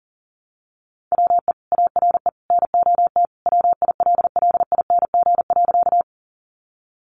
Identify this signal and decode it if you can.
Morse code, heard as a keyed tone, about 40 wpm, 700 Hz. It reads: WE ARE NOT WILLING.
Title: Sentences from Top 1000 Words 40wpm